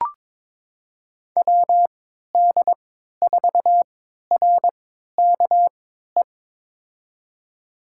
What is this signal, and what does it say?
Morse code, heard as a keyed tone, about 21 wpm, spaced as for 12 wpm, 700 Hz. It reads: WD4RKE
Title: USA Call Signs - 6 Characters - 3x Character Spacing - 22wpm